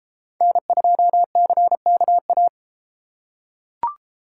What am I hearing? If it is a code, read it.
Morse code, heard as a keyed tone, about 31 wpm, 700 Hz. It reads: N2CKA